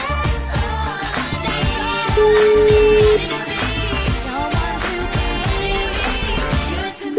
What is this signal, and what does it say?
Ringback tone (Europe, 425 Hz), with noise.